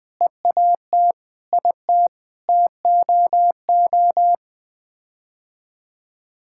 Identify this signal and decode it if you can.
Morse code, heard as a keyed tone, about 20 wpm, 700 Hz. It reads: EAT IT TOO
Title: Sets of 8 Words - 20wpm